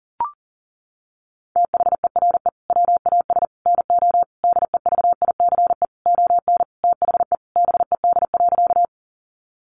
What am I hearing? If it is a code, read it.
Morse code, heard as a keyed tone, about 40 wpm, 700 Hz. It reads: THERE WAS NO DEVICE ON THE BED.
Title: Sentences from Top 1500 Words 40wpm